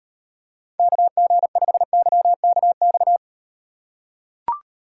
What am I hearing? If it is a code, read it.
Morse code, heard as a keyed tone, about 38 wpm, 700 Hz. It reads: KG5YKX